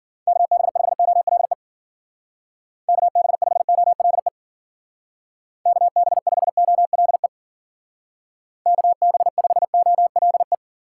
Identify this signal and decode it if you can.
Morse code, heard as a keyed tone, about 50 wpm, 700 Hz. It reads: KB5OLE KB5OLE KB5OLE KB5OLE